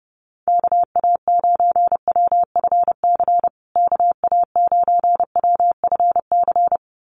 Morse code, heard as a keyed tone, about 30 wpm, 700 Hz. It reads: KA9WFC KA9WFC
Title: USA Call Signs Repeated 2x 30wpm